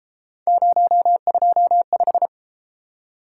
Morse code, heard as a keyed tone, about 31 wpm, 700 Hz. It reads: 025